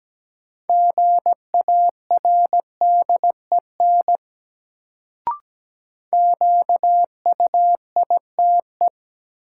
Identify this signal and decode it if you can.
Morse code, heard as a keyed tone, about 17 wpm, 700 Hz. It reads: GARDEN QUITE
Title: Sentences from Top 700 Words - New Word Drill - 17wpm